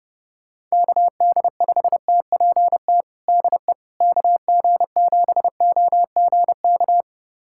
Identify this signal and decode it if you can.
Morse code, heard as a keyed tone, about 30 wpm, 700 Hz. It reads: KD5TPT DE KG7OGK
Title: USA Callsign DE Callsign 30wpm